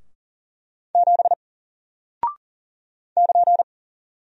Morse code, heard as a keyed tone, about 40 wpm, 700 Hz. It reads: 7 <KN>
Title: Single Character (Letter:Number:Punctuation:Prosign) Alt 40wpm